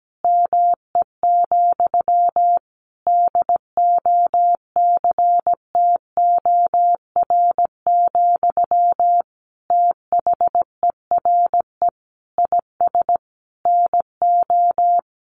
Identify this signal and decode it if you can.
Morse code, heard as a keyed tone, about 17 wpm, 700 Hz. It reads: ME, DOCTOR, THERE IS NO
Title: Sentences from Top 1500 Words 17wpm